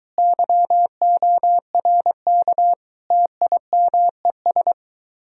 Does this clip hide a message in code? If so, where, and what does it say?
Yes: Morse code from 0.2 to 4.7 s, heard as a keyed tone, about 23 wpm, 700 Hz. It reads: YORK TIMES